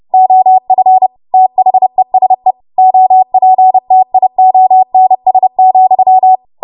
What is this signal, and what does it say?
Morse code, heard as a keyed tone, about 30 wpm, 750 Hz. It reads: OF THESE OPTIONS,
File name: Morse / 30WPM